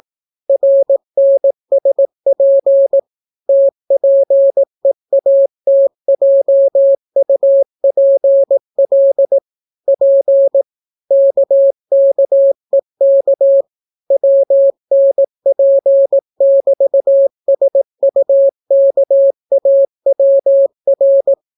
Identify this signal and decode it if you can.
Morse code, heard as a keyed tone, about 18 wpm, 550 Hz. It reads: RNSP TPEATJUPL P KKEK WNP<BT>SUKAWR